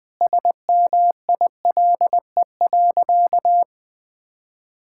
Morse code, heard as a keyed tone, about 20 wpm, 700 Hz. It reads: SMILE.
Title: Sentences with 8-Word Phrases - 20wpm